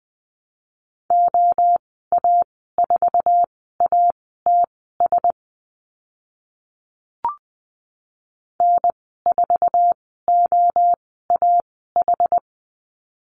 Morse code, heard as a keyed tone, about 20 wpm, spaced as for 15 wpm, 700 Hz. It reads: OA4ATS N4OAH
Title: 062 Lesson 11g - Letter h - Call Signs 20wpm